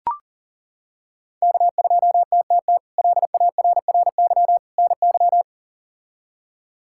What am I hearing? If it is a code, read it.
Morse code, heard as a keyed tone, about 40 wpm, 700 Hz. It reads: K2TTT LARRY NY
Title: K1USN SST QSO Practice - 40wpm